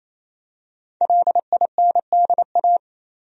Morse code, heard as a keyed tone, about 28 wpm, 700 Hz. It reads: LINDA
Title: USA Common Names - 28wpm